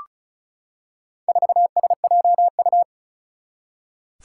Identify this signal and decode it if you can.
Morse code, heard as a keyed tone, about 36 wpm, 700 Hz. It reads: 4SJU
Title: Four Challenging Letter-Number Combinations 35wpm